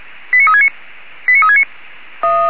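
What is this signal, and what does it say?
Five-tone selective call (ZVEI), heard as quick runs of five tones, tones of about 70 ms. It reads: 89369 89369
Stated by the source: ZVEI five-tone call 89369, received by SDR.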